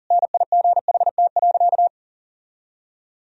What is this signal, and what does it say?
Morse code, heard as a keyed tone, about 40 wpm, 700 Hz. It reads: NIGHT.
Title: Sentences from Top 1000 Words - Review Part 2 - 40wpm